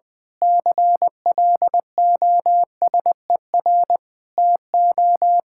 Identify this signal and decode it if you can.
Morse code, heard as a keyed tone, about 20 wpm, 700 Hz. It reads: CLOSER TO
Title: Pangram Challenge v2 - 20wpm